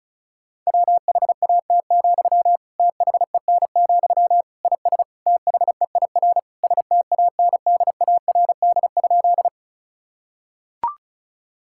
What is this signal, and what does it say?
Morse code, heard as a keyed tone, about 36 wpm, 700 Hz. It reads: WHAT, THEN, IS THEIR STANDARD?
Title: Sentences from Top 1000 Words - Review Part 6 - 35wpm